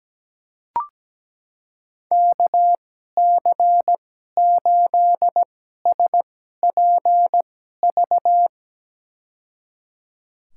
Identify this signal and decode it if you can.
Morse code, heard as a keyed tone, about 17 wpm, spaced as for 12 wpm, 700 Hz. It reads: KC8SPV
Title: USA Call Signs - 2x Character Spacing - 17wpm